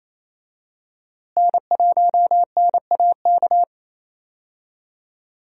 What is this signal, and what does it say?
Morse code, heard as a keyed tone, about 28 wpm, 700 Hz. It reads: N1NAK
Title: USA Call Signs - 5 Characters - Repeated 2x - 28wpm